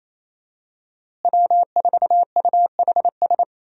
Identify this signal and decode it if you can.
Morse code, heard as a keyed tone, about 28 wpm, 700 Hz. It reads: W4UHS
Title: USA Call Signs - 5 Characters - Repeated 2x - 28wpm